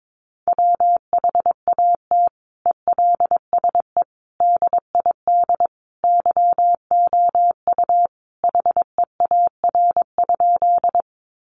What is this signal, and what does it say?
Morse code, heard as a keyed tone, about 22 wpm, 700 Hz. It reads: WHAT ELSE DID YOU HEAR?
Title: Sentences from Top 800 Words 22wpm